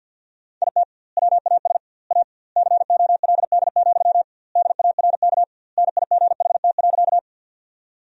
Morse code, heard as a keyed tone, about 48 wpm, 700 Hz. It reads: IT WAS A COLD, DARK NIGHT.
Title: 2x Word-Spacing - Sentences from Top 500 Words - 50wpm